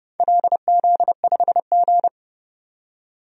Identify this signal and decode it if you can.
Morse code, heard as a keyed tone, about 30 wpm, 700 Hz. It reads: LZ5G
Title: DXCC-100 Call Signs - 4 Characters - 30wpm